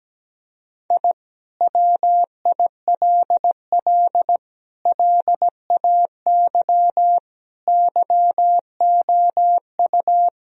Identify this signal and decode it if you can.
Morse code, heard as a keyed tone, about 17 wpm, 700 Hz. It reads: I WILL LAY YOU
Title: Sentences from Top 1000 Words - Review Part 2 - 17wpm